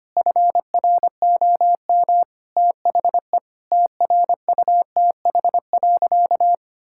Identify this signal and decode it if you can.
Morse code, heard as a keyed tone, about 25 wpm, 700 Hz. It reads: FROM THE TRUTH.
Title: Sentences from Top 1600 Words 25wpm